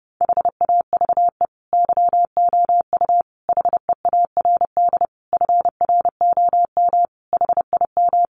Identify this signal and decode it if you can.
Morse code, heard as a keyed tone, about 30 wpm, 700 Hz. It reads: HAVE YOU HEARD FROM HIM
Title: Sentences from Top 400 Words 30wpm